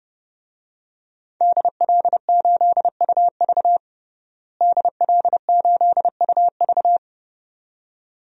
Morse code, heard as a keyed tone, about 30 wpm, 700 Hz. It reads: DL8UV DL8UV
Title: DXCC-100 Call Signs - 5 Characters - Repeated 2x - 30wpm